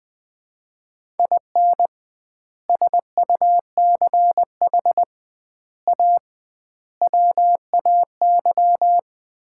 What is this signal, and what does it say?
Morse code, heard as a keyed tone, about 21 wpm, 700 Hz. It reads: IN SUCH A WAY
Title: 2x Word-Spacing - Sets of 5 Words - 20wpm